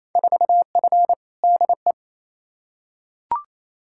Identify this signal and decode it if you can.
Morse code, heard as a keyed tone, about 28 wpm, 700 Hz. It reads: 4F DE
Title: Field Day Exchanges Encore - 28wpm